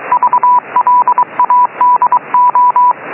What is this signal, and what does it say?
Morse code, heard as a keyed tone, about 23 wpm, 990 Hz, with radio noise. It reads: VLADO